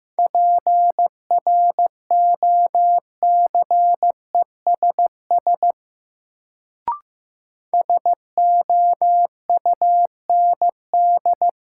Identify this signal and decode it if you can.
Morse code, heard as a keyed tone, about 15 wpm, 700 Hz. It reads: PROCESS SOUND
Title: Sets of 3 Words - New Word Drill - 15wpm